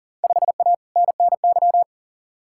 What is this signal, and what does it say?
Morse code, heard as a keyed tone, about 40 wpm, 700 Hz. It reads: 5A NNY
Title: Field Day Exchanges - 40wpm